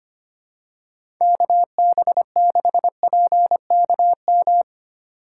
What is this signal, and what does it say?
Morse code, heard as a keyed tone, about 25 wpm, 700 Hz. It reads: KB6PKM
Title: USA Call Signs - 4x Character Spacing - 25wpm